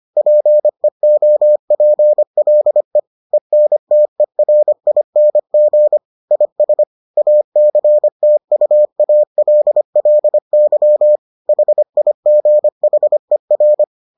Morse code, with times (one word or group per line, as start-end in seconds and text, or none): 0.2-3.0 PEOPLE
3.3-6.0 ENTERING
6.3-6.8 IS
7.2-11.2 ACTUALLY
11.5-13.8 HIGHER